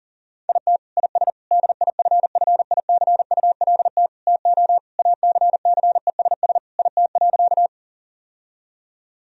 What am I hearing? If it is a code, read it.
Morse code, heard as a keyed tone, about 40 wpm, 700 Hz. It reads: IT IS DIFFICULT TO ACCESS IT.
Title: Sentences from Top 1000 Words - Review Part 4 - 40wpm